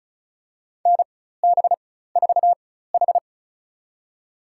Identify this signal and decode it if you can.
Morse code, heard as a keyed tone, about 34 wpm, spaced as for 16 wpm, 700 Hz. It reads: NB4H
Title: USA Call Signs - 4 Characters - 4x Character Spacing - 35wpm